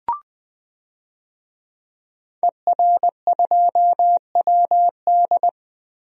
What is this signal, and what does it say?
Morse code, heard as a keyed tone, about 20 wpm, 700 Hz. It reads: ER2WD
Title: DXCC-100 Call Signs - 5 Characters - Repeated 3x - 20wpm